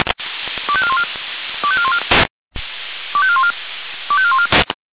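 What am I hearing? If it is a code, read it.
Five-tone selective call (ZVEI), heard as quick runs of five tones, tones of about 70 ms. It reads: 25414 25414 25414 25414
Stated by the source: ZVEI five-tone selective call received by SDR.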